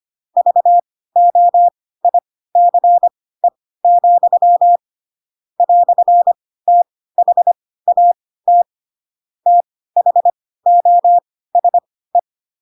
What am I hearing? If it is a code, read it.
Morse code, heard as a keyed tone, about 25 wpm, spaced as for 16 wpm, 700 Hz. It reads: VOICE, "THAT THOSE
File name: Pinocchio / Pinocchio ch4 10 25 ik1hge